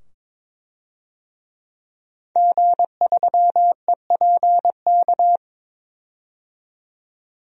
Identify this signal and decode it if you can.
Morse code, heard as a keyed tone, about 22 wpm, 700 Hz. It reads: G3EPK
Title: DXCC-100 Call Signs - 5 Characters - 22wpm